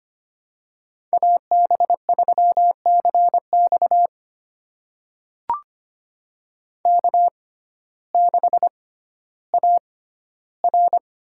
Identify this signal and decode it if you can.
Morse code, heard as a keyed tone, about 25 wpm, spaced as for 14 wpm, 700 Hz. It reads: AB3CX K6AR
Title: USA Call Signs - 5 Characters - 6x Character Spacing - 25wpm